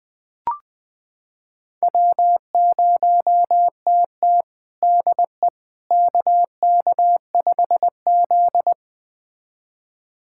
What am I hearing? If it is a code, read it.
Morse code, heard as a keyed tone, about 20 wpm, 700 Hz. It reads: W0TT DE KK5Z
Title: USA Callsign DE Callsign - 4 Characters - 20wpm